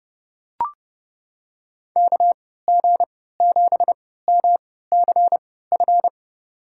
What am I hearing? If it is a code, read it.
Morse code, heard as a keyed tone, about 29 wpm, spaced as for 17 wpm, 700 Hz. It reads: KG7MCF
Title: USA Call Signs - 6 Characters - 3x Character Spacing - 30wpm